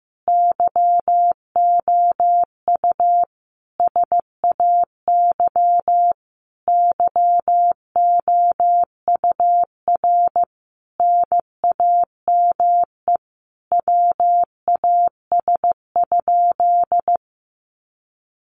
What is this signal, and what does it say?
Morse code, heard as a keyed tone, about 15 wpm, 700 Hz. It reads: YOU SAY YOUR NAME WAS?
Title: Sentences from Top 200 Words 15wpm